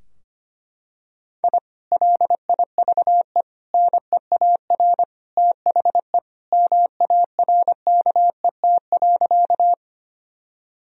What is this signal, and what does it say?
Morse code, heard as a keyed tone, about 25 wpm, 700 Hz. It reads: I LIVE NEAR THE MARKET.
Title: Sentences from Top 1000 Words - Review Part 3 - 25wpm